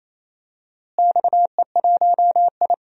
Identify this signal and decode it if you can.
Morse code, heard as a keyed tone, about 27 wpm, 700 Hz. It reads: XE1I
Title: DXCC-100 Call Signs - 4 Characters - Repeated 3x - 28wpm